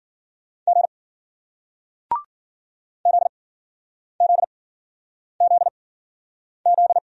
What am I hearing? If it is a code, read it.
Morse code, heard as a keyed tone, about 47 wpm, spaced as for 9 wpm, 700 Hz. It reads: G ZZZZ
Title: Speed-Racing Single Letter-Number 40wpm